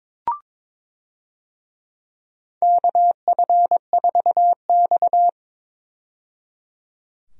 Morse code, heard as a keyed tone, about 22 wpm, 700 Hz. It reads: KF4X
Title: USA Call Signs - 4 Characters - 22wpm